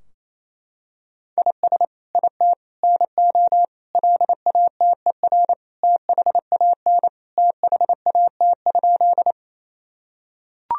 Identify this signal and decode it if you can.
Morse code, heard as a keyed tone, about 27 wpm, 700 Hz. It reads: IS IT NO LATER THAN THAT?
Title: Sentences from Top 1000 Words - Review Part 4 - 28wpm